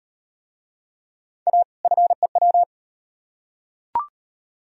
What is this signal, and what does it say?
Morse code, heard as a keyed tone, about 38 wpm, 700 Hz. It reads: A FEW